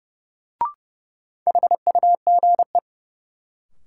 Morse code, heard as a keyed tone, about 31 wpm, 700 Hz. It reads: HUGE
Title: Sentences from Top 1100 Words - New Word Drill - 30wpm